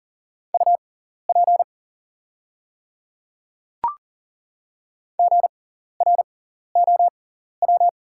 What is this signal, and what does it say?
Morse code, heard as a keyed tone, about 40 wpm, spaced as for 14 wpm, 700 Hz. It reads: UP GROW
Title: 6x Character Spacing - Top 500 Words - 40wpm